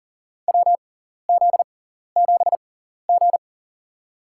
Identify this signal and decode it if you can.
Morse code, heard as a keyed tone, about 40 wpm, spaced as for 14 wpm, 700 Hz. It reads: WZ7G
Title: USA Call Signs - 6x Character Spacing - 40wpm